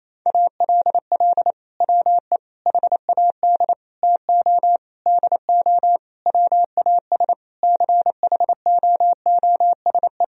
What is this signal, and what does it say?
Morse code, heard as a keyed tone, about 28 wpm, 700 Hz. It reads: ALL WE HAD TO DO WAS CHOOSE
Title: Sentences from Top 1000 Words - Review Part 4 - 28wpm